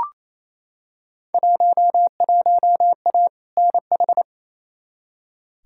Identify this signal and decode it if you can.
Morse code, heard as a keyed tone, about 28 wpm, 700 Hz. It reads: 11A NH
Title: Field Day Exchanges Encore - 28wpm